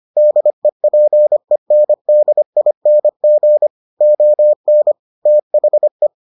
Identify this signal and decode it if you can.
Morse code, heard as a keyed tone, about 25 wpm, 600 Hz. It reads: DEPENDING ON THE